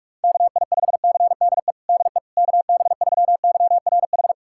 Morse code, heard as a keyed tone, about 46 wpm, 700 Hz. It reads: KI5CDE DE KB3YRH
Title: USA Callsign DE Callsign 45wpm